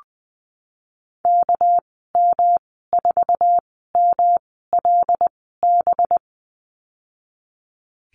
Morse code, heard as a keyed tone, about 20 wpm, spaced as for 14 wpm, 700 Hz. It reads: KM4MLB
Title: 238 Lesson 38g - Letter q - Call Signs 20wpm